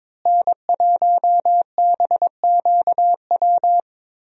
Morse code, heard as a keyed tone, about 22 wpm, 700 Hz. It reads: N1BQW
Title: USA Call Signs - 5 Characters - Repeated 2x - 22wpm